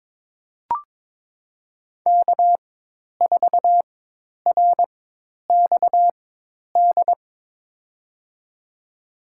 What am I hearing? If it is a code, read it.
Morse code, heard as a keyed tone, about 22 wpm, spaced as for 10 wpm, 700 Hz. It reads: K4RXD
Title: USA Call Signs - 4x Character Spacing - 22wpm